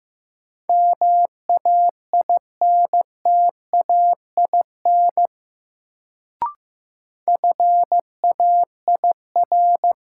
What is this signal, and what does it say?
Morse code, heard as a keyed tone, about 15 wpm, 700 Hz. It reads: MAINTAIN FAIR
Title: Sentences from Top 1000 Words - Review Part 5 - New Word Drill - 15wpm